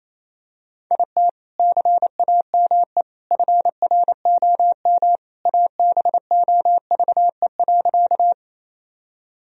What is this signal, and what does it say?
Morse code, heard as a keyed tone, about 28 wpm, 700 Hz. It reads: IT CAME FROM ABOVE.
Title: Sentences from Top 1000 Words - Review Part 2 - 28wpm